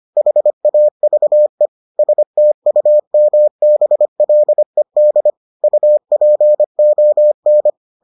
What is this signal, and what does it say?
Morse code, heard as a keyed tone, about 25 wpm, 600 Hz. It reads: HAVE STUMBLED UPON